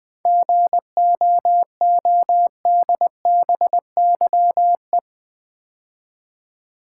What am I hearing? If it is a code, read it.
Morse code, heard as a keyed tone, about 20 wpm, 700 Hz. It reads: GOODBYE
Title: 7 Letter Words - First 3 Letters Given - 20wpm